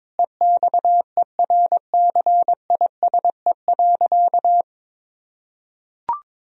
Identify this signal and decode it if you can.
Morse code, heard as a keyed tone, about 22 wpm, 700 Hz. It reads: EXERCISE.
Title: Sentences from Top 1000 Words - Review Part 4 - 22wpm